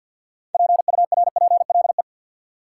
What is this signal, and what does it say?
Morse code, heard as a keyed tone, about 53 wpm, 700 Hz. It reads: PURPLE
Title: Six Letter Words - 50wpm